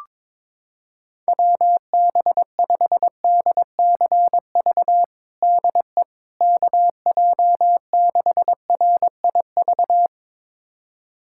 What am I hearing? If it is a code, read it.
Morse code, heard as a keyed tone, about 21 wpm, 700 Hz. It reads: WB5DCV DE KJ6RIV
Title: USA Callsign DE Callsign - 6 Characters - 22wpm